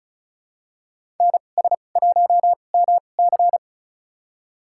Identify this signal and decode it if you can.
Morse code, heard as a keyed tone, about 36 wpm, spaced as for 25 wpm, 700 Hz. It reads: NS1MC
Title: USA Call Signs - 5 Characters - 2x Character Spacing - 35wpm